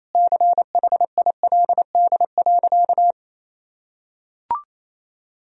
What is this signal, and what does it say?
Morse code, heard as a keyed tone, about 28 wpm, 700 Hz. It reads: CHILD.
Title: Sentences from Top 1000 Words - Review Part 1 - 28wpm